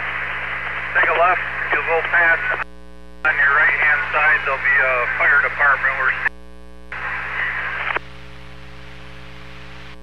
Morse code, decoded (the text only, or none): none